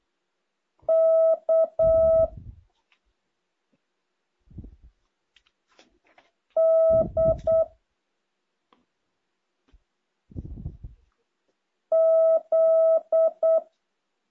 Morse code, heard as a keyed tone, about 8 wpm, 650 Hz. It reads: K D Z
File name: jh8cbh / 2023cw